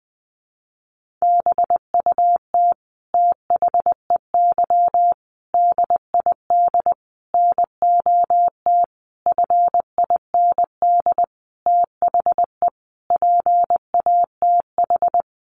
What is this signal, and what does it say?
Morse code, heard as a keyed tone, about 19 wpm, 700 Hz. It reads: BUT THEY DID NOT FIND THE PATH
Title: Sentences from Top 1400 Words 20wpm